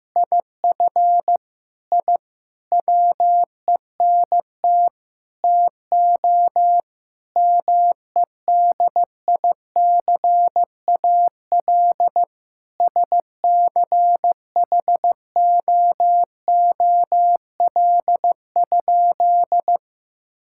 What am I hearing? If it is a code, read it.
Morse code, heard as a keyed tone, about 15 wpm, 700 Hz. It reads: IF I WENT TO MEDICAL SCHOOL?
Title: Sentences from Top 1000 Words - Review Part 5 - 15wpm